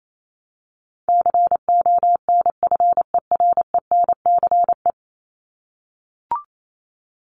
Morse code, heard as a keyed tone, about 28 wpm, 700 Hz. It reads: CONFERENCE
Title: Top 1751-2000 Words 28wpm